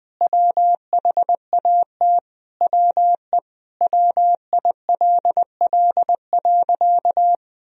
Morse code, heard as a keyed tone, about 21 wpm, 700 Hz. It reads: WHAT WE WILL.
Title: Sentences from Top 1000 Words - Review Part 1 - 20wpm